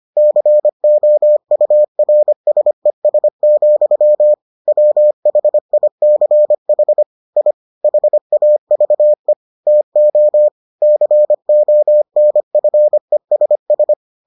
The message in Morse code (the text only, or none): COURSES, WHICH I HAVE TO CONFESS